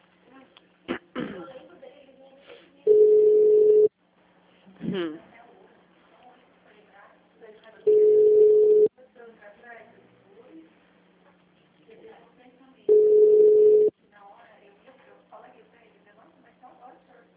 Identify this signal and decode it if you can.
Ringback tone (Europe, 425 Hz), with speech.